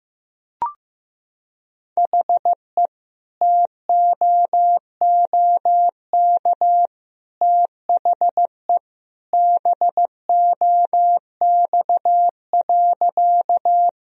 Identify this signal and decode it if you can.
Morse code, heard as a keyed tone, about 15 wpm, 700 Hz. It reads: HE TOOK THE BOX.
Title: Sentences from Top 1000 Words - Review Part 2 - 15wpm